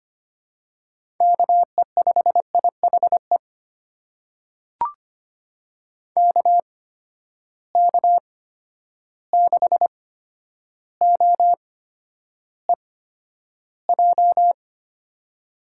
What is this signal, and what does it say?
Morse code, heard as a keyed tone, about 25 wpm, spaced as for 11 wpm, 700 Hz. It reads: KE5IHE KK6OEJ
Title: USA Call Signs - 6 Characters - 8x Character Spacing - 25wpm